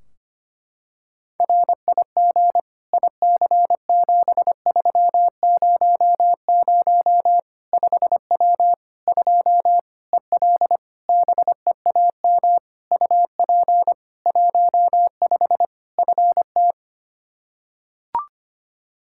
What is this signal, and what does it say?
Morse code, heard as a keyed tone, about 25 wpm, 700 Hz. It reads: RIG IC7300 5W 2 EL BEAM UP 15 FT
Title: K1USN SST QSO Practice 25wpm